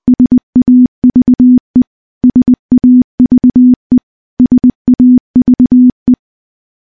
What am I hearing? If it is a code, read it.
Morse code, heard as a keyed tone, about 21 wpm, 250 Hz. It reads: SAVE SAVE SAVE
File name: morse-training-20WPM